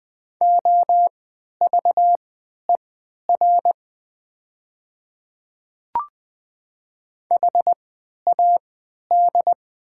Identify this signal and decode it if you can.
Morse code, heard as a keyed tone, about 20 wpm, spaced as for 11 wpm, 700 Hz. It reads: OVER HAD
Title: 3x Character Spacing - Top 500 Words - 20wpm